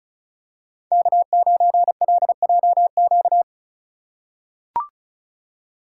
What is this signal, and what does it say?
Morse code, heard as a keyed tone, about 36 wpm, 700 Hz. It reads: K9LJQ
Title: USA Call Signs - 5 Characters - 4x Character Spacing - 35wpm